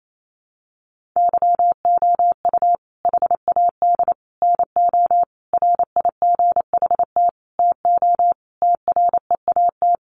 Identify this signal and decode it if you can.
Morse code, heard as a keyed tone, about 28 wpm, 700 Hz. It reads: YOU HAD NO RIGHT TO TREAT
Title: Sentences from Top 1300 Words 28wpm